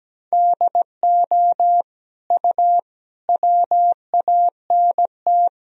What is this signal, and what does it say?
Morse code, heard as a keyed tone, about 17 wpm, 700 Hz. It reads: DO U WANT
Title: Sentences from Top 300 Words and Abbreviations 17wpm